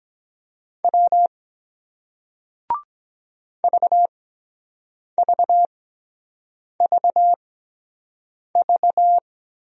Morse code, heard as a keyed tone, about 24 wpm, spaced as for 7 wpm, 700 Hz. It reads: W VVVV